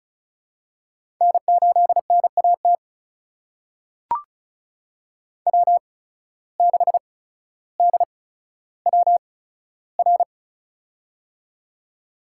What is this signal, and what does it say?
Morse code, heard as a keyed tone, about 36 wpm, spaced as for 15 wpm, 700 Hz. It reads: N8NAT W6DWR
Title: USA Call Signs - 5 Characters - 8x Character Spacing - 35wpm